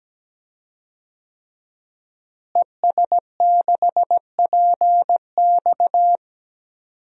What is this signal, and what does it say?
Morse code, heard as a keyed tone, about 17 wpm, 700 Hz. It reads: ES6PX
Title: DXCC-100 Call Signs - 5 Characters - Repeated 3x - 17wpm